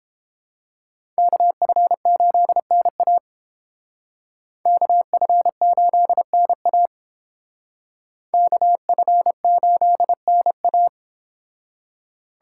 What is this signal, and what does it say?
Morse code, heard as a keyed tone, about 30 wpm, 700 Hz. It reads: KF8NA KF8NA KF8NA